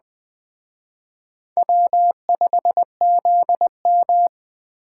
Morse code, heard as a keyed tone, about 21 wpm, 700 Hz. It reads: W5ZM
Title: Speed Racing - USA 1x2 Call Signs 17wpm